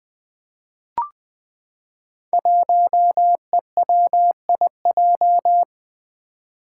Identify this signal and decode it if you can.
Morse code, heard as a keyed tone, about 20 wpm, 700 Hz. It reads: 1EWIJ
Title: Five Challenging Letter-Number Combinations 20wpm